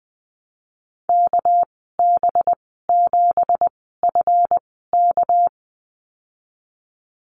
Morse code, heard as a keyed tone, about 19 wpm, spaced as for 14 wpm, 700 Hz. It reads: KB7FK
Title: 200 Lesson 32g - Letter k - Call Signs 20wpm